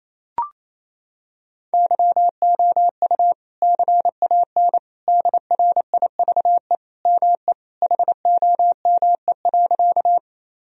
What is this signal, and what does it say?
Morse code, heard as a keyed tone, about 28 wpm, 700 Hz. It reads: YOU CAN DRIVE ME HOME.
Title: Sentences from Top 1000 Words - Review Part 2 - 28wpm